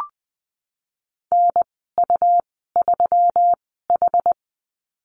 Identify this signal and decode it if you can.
Morse code, heard as a keyed tone, about 21 wpm, spaced as for 15 wpm, 700 Hz. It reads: NU3H
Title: 122 Lesson 20g - Number 3 - Call Signs 20wpm